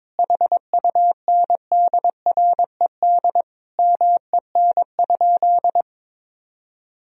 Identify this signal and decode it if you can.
Morse code, heard as a keyed tone, about 22 wpm, 700 Hz. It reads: HUNDRED MEN?
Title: Sentences from Top 1000 Words - Review Part 2 - 22wpm